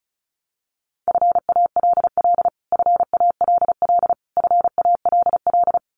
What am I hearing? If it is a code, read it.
Morse code, heard as a keyed tone, about 34 wpm, 700 Hz. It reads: FALL FALL FALL
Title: Top 500 Words Repeated 35wpm